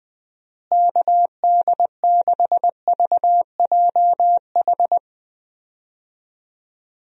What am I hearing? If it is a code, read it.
Morse code, heard as a keyed tone, about 21 wpm, 700 Hz. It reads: KD6VJH
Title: USA Call Signs - 6 Characters - Repeated 2x - 20wpm